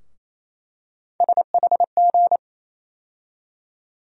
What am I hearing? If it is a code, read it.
Morse code, heard as a keyed tone, about 28 wpm, 700 Hz. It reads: SHG
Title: USA Call Sign Postfixes - 3 Letters 28wpm